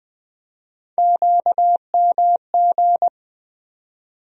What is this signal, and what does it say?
Morse code, heard as a keyed tone, about 20 wpm, 700 Hz. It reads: QMG
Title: USA Call Sign Postfixes - 3 Letters 20wpm